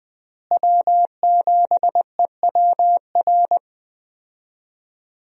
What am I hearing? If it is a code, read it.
Morse code, heard as a keyed tone, about 21 wpm, 700 Hz. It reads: W7EWR
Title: USA Call Signs - 5 Characters - Repeated 2x - 20wpm